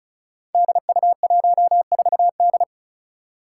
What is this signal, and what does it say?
Morse code, heard as a keyed tone, about 34 wpm, 700 Hz. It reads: DU14D
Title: Five Challenging Letter-Number Combinations 35wpm